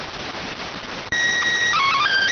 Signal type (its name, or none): SSTV (slow-scan television)